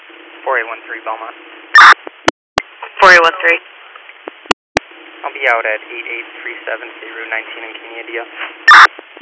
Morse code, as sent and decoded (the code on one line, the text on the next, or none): none